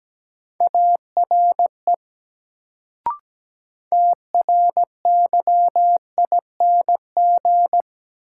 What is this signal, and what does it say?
Morse code, heard as a keyed tone, about 17 wpm, 700 Hz. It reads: ARE TRYING
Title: Sets of 2 Words - New Word Drill - 17wpm